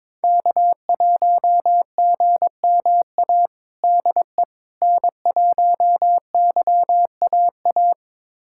Morse code, heard as a keyed tone, about 21 wpm, 700 Hz. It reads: K1GMA DE N1YAA
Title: USA Callsign DE Callsign - 5 Characters - 22wpm